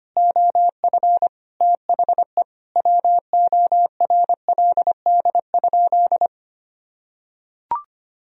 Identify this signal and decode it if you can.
Morse code, heard as a keyed tone, about 25 wpm, 700 Hz. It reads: OF THE WORLD?
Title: Sentences from Top 200 Words and Abbreviations 25wpm